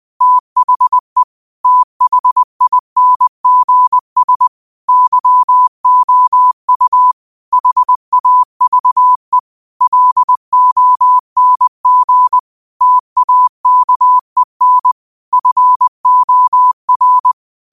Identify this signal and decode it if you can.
Morse code, heard as a keyed tone, about 20 wpm, 1000 Hz. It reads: THE THINGS YOU HAVE LONG TAKEN FOR
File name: qotdfiles / 20wpm